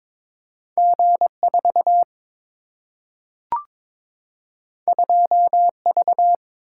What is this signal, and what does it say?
Morse code, heard as a keyed tone, about 22 wpm, 700 Hz. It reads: G 4 2 V
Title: Single-Letter Single-Number 22wpm